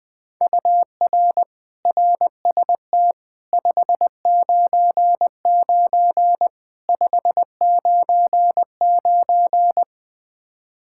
Morse code, heard as a keyed tone, about 20 wpm, 700 Hz. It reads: UR RST 599 599
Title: Practice QSO - Exchange 1 - Beginning - 20wpm